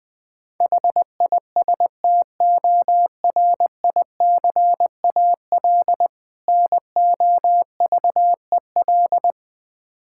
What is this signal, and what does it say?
Morse code, heard as a keyed tone, about 20 wpm, 700 Hz. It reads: HISTORICAL NOVEL
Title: Difficult Words with Hints - 20wpm